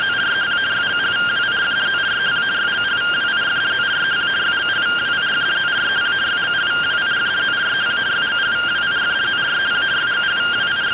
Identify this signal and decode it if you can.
Two-tone teleprinter signal, about 170 Hz shift.
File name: findings / axial iodine 3mm 205.gen-11 rtty rx